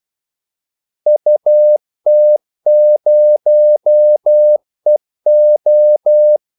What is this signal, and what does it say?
Morse code, heard as a keyed tone, about 12 wpm, 600 Hz. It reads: UT0EO
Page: 2 ekstra ordmellemrum mellem kaldesignalerne